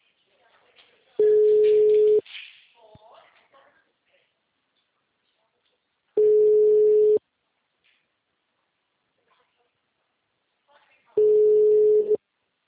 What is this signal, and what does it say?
Ringback tone (Europe, 425 Hz), with speech.
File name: out- -2053